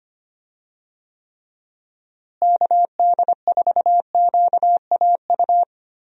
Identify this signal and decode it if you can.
Morse code, heard as a keyed tone, about 25 wpm, 700 Hz. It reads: KD4QAU
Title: USA Call Signs - 6 Characters - Repeated 3x - 25wpm